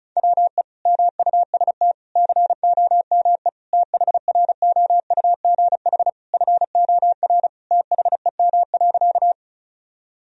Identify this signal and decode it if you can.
Morse code, heard as a keyed tone, about 34 wpm, 700 Hz. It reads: WE MUST COME THROUGH FOR THEM.
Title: Sentences from Top 1000 Words - Review Part 1 - 35wpm